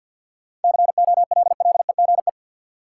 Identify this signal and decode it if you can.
Morse code, heard as a keyed tone, about 53 wpm, 700 Hz. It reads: COLLEGE
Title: Sentences from Top 1300 Words - New Word Drill - 50wpm